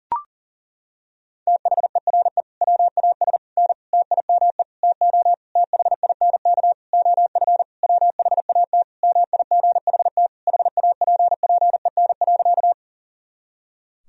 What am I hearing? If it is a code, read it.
Morse code, heard as a keyed tone, about 40 wpm, 700 Hz. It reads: THERE WAS N TIME TO THINK OF WHAT MIGHT HAPPEN.
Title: Sentences from Top 400 Words and Abbreviations 40wpm